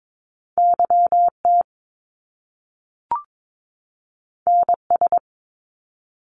Morse code, heard as a keyed tone, about 22 wpm, 700 Hz. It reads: YT NS
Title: Canadian Provinces and Territories 22wpm